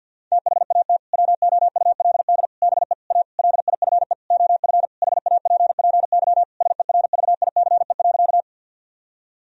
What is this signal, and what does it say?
Morse code, heard as a keyed tone, about 50 wpm, 700 Hz. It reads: THAT WOULD BE A LIFE OF HAPPY SERVICE.
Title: Sentences from Top 1000 Words - Review Part 4 - 50wpm